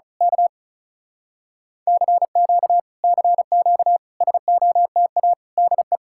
Morse code, heard as a keyed tone, about 33 wpm, 700 Hz. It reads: K CQ CQ SOTA DE
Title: Speed-Racing Common QSO Phrases Encore - 35wpm